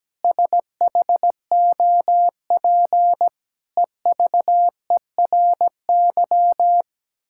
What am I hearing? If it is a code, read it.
Morse code, heard as a keyed tone, about 17 wpm, 700 Hz. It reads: SHOP EVERY
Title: Sentences from Top 1600 Words 17wpm